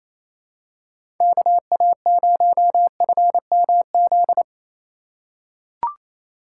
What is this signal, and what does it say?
Morse code, heard as a keyed tone, about 28 wpm, 700 Hz. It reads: KA0FMZ
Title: USA Call Signs - 2x Character Spacing - 28wpm